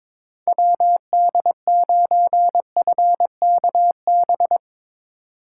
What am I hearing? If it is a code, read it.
Morse code, heard as a keyed tone, about 22 wpm, 700 Hz. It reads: WD9FKB
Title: Speed Racing - USA 2x3 Call Signs 22wpm